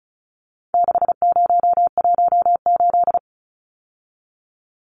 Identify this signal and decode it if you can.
Morse code, heard as a keyed tone, about 34 wpm, 700 Hz. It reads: 6018
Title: Numbers - 4-Digit 35wpm